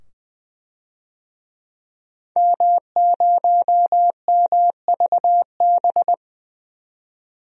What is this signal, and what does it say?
Morse code, heard as a keyed tone, about 20 wpm, 700 Hz. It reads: M0MVB
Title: DXCC-100 Call Signs - 5 Characters - 20wpm